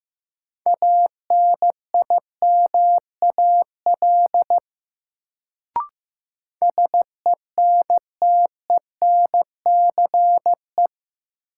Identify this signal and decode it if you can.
Morse code, heard as a keyed tone, about 15 wpm, 700 Hz. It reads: ANIMAL SENTENCE
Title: Sentences from Top 1000 Words - Review Part 1 - New Word Drill - 15wpm